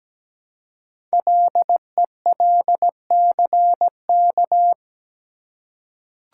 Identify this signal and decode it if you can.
Morse code, heard as a keyed tone, about 17 wpm, 700 Hz. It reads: LELCK
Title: Code Groups - 5 Letters - 17wpm